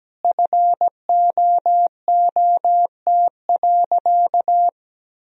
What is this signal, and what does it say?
Morse code, heard as a keyed tone, about 17 wpm, 700 Hz. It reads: FOOT.
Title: Sentences from Top 1000 Words - Review Part 3 - 17wpm